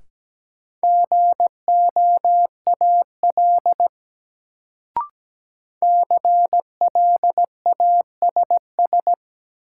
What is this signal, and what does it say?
Morse code, heard as a keyed tone, about 17 wpm, 700 Hz. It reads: GOAL CLASS
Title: Sentences from Top 1400 Words - New Word Drill Alt - 17wpm